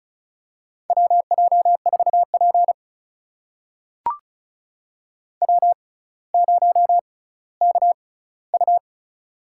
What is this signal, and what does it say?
Morse code, heard as a keyed tone, about 34 wpm, spaced as for 18 wpm, 700 Hz. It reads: WJ4P W0KU
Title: USA Call Signs - 6x Character Spacing - 35wpm